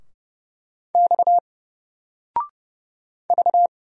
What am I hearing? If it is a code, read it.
Morse code, heard as a keyed tone, about 30 wpm, 700 Hz. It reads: X V
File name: Single-Letter-Number-30wpm